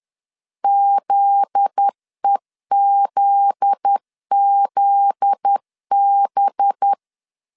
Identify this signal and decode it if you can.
Morse code, heard as a keyed tone, about 10 wpm, 800 Hz. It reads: ZEZZB
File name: LEZIONI / 7 GF9Z / 7 GF9Z 10wpm 2611